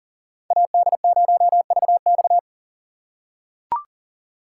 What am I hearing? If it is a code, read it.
Morse code, heard as a keyed tone, about 40 wpm, 700 Hz. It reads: AD0VX
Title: USA Call Signs - 5 Characters - 2x Character Spacing - 40wpm